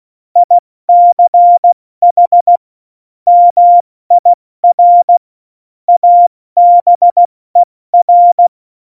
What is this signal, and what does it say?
Morse code, heard as a keyed tone, about 16 wpm, spaced as for 14 wpm, 700 Hz. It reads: ICH MIR ABER
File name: Morsekurs / Witze / Witz 13. Oktober 2020, 70 F80